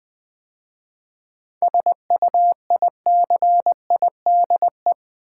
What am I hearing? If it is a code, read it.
Morse code, heard as a keyed tone, about 20 wpm, 700 Hz. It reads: SUICIDE
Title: 7 Letter Words - First 3 Letters Given - 20wpm